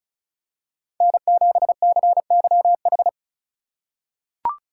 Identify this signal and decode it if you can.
Morse code, heard as a keyed tone, about 36 wpm, 700 Hz. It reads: N7CYH
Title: USA Call Signs - 4x Character Spacing - 35wpm